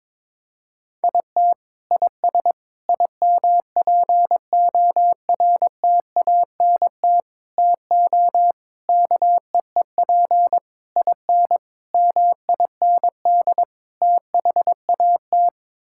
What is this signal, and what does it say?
Morse code, heard as a keyed tone, about 21 wpm, 700 Hz. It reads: IT IS IMPORTANT TO KEEP IN MIND THAT
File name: practice / 22wpm / Sets of 8 Words - 22wpm